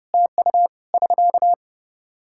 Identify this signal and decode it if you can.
Morse code, heard as a keyed tone, about 30 wpm, 700 Hz. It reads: TU <SK>
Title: Practice QSO - Exchange 4 - Ending - 30wpm